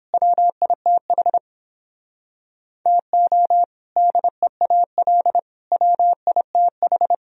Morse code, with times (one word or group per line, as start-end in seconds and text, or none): 0.1-1.4 WITH
2.9-3.6 TO
4.0-5.4 DEAL
5.7-7.1 WITH